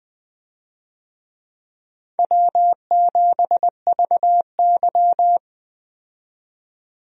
Morse code, heard as a keyed tone, about 20 wpm, 700 Hz. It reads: W7VY
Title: USA Call Signs - 4 Characters - 20wpm